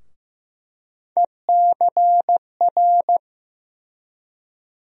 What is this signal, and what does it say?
Morse code, heard as a keyed tone, about 15 wpm, 700 Hz. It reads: ECR
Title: USA Call Sign Postfixes - 3 Letters 15wpm